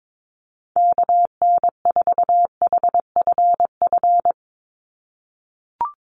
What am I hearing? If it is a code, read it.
Morse code, heard as a keyed tone, about 22 wpm, 700 Hz. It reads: KN4HFF
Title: USA Call Signs 22wpm